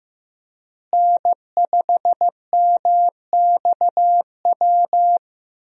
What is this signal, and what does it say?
Morse code, heard as a keyed tone, about 15 wpm, 700 Hz. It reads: N5MXW
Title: USA Call Signs - 5 Characters - Repeated 3x - 15wpm